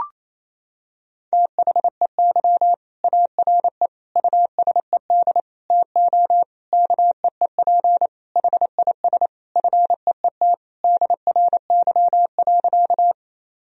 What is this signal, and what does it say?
Morse code, heard as a keyed tone, about 28 wpm, 700 Hz. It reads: THEY ARE USED TO KEEP HIS FEET DRY.
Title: Sentences from Top 1000 Words - Review Part 3 - 28wpm